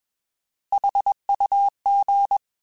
Morse code, heard as a keyed tone, about 21 wpm, 800 Hz. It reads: HUG
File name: Alphabetical / Alphabetical mp3s / Hs @20